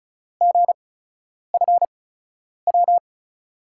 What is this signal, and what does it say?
Morse code, heard as a keyed tone, about 36 wpm, spaced as for 10 wpm, 700 Hz. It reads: GFW
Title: USA Call Signs - 5 Characters - 8x Character Spacing - 35wpm